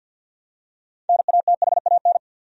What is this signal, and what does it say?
Morse code, heard as a keyed tone, about 53 wpm, 700 Hz. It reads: NATHAN
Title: USA Common Names - 50wpm